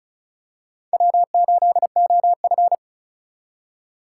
Morse code, heard as a keyed tone, about 36 wpm, 700 Hz. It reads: W8OF
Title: USA Call Signs - 4 Characters - 4x Character Spacing - 35wpm